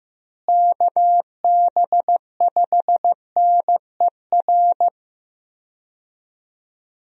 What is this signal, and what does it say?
Morse code, heard as a keyed tone, about 15 wpm, 700 Hz. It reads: KB5NER
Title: USA Call Signs - 6 Characters - Repeated 2x - 15wpm